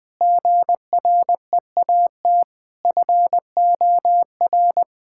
Morse code, heard as a keyed tone, about 21 wpm, 700 Hz. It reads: GREAT FOR
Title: Sentences from Top 1000 Words - Review Part 5 - 20wpm